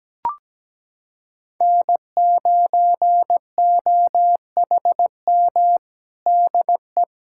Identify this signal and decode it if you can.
Morse code, heard as a keyed tone, about 17 wpm, 700 Hz. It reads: N9OHM DE
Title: USA Callsign DE Callsign 17wpm